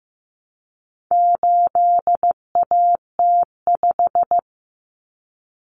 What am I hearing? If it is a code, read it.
Morse code, heard as a keyed tone, about 15 wpm, 700 Hz. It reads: 8AT5
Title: Common Cut Numbers - 4-Digit 15wpm